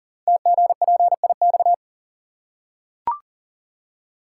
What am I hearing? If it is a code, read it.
Morse code, heard as a keyed tone, about 39 wpm, 700 Hz. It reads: TGPIX
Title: Code Groups - 5 Letters - 40wpm